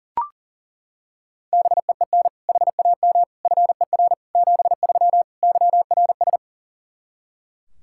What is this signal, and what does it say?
Morse code, heard as a keyed tone, about 37 wpm, 700 Hz. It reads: BEEN HAM FER 73 YRS
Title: K1USN SST QSO Practice 40wpm